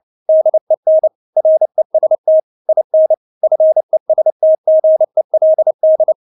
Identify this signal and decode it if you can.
Morse code, heard as a keyed tone, about 29 wpm, 630 Hz. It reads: DEN REST IN FESTGELD
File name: Witz10 29wpm